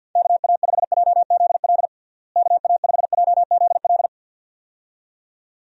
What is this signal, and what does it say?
Morse code, heard as a keyed tone, about 49 wpm, 700 Hz. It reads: KA5JZL KA5JZL
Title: USA Call Signs - 6 Characters - Repeated 2x - 50wpm